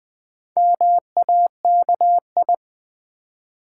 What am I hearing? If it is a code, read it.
Morse code, heard as a keyed tone, about 20 wpm, 700 Hz. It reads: MAKI
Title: Build A Word - 2 Letters At A Time - 6-Letter Words - 20wpm